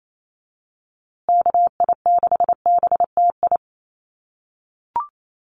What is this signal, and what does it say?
Morse code, heard as a keyed tone, about 28 wpm, 700 Hz. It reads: KI6BTI
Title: USA Call Signs Repeated 2x 28wpm